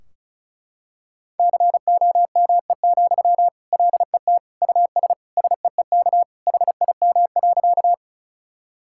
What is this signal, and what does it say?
Morse code, heard as a keyed tone, about 36 wpm, 700 Hz. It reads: COME, LET US SEEK HIM.
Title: Sentences from Top 1600 Words 35wpm